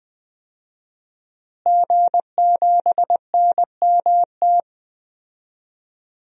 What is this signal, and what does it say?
Morse code, heard as a keyed tone, about 20 wpm, 700 Hz. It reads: G7NMT
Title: DXCC-100 Call Signs - 5 Characters - Repeated 3x - 20wpm